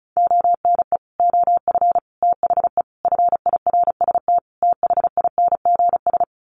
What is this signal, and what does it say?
Morse code, heard as a keyed tone, about 34 wpm, 700 Hz. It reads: ONE OF THE FIRST THINGS
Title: Sets of 5 Words Encore 35wpm